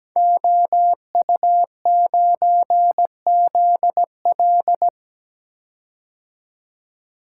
Morse code, heard as a keyed tone, about 17 wpm, 700 Hz. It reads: OU9ZL
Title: Code Groups - 5 Letter:Numbers - 17wpm